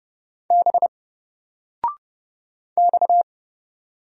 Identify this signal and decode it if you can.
Morse code, heard as a keyed tone, about 29 wpm, 700 Hz. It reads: B X
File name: Single-Letter-Number-30wpm